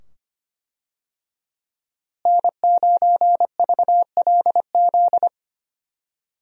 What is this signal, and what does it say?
Morse code, heard as a keyed tone, about 25 wpm, 700 Hz. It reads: N9VLZ
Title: USA Call Signs - 5 Characters - Repeated 2x - 25wpm